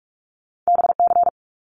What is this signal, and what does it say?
Morse code, heard as a keyed tone, about 44 wpm, 700 Hz. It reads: BC
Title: Canadian Provinces and Territories - Repeated 3x - 45wpm